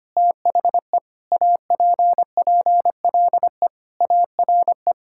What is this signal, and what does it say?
Morse code, heard as a keyed tone, about 25 wpm, 700 Hz. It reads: THE APPLE ARE
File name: practice / 25wpm / Sentences with 3-Word Phrases - 25wpm